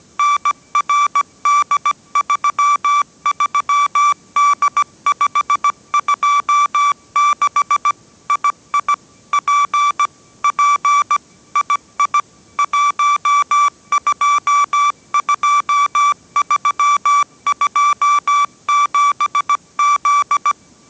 Morse code, heard as a keyed tone, about 17 wpm, 1160 Hz. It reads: NRD33D526 II P P II 122327Z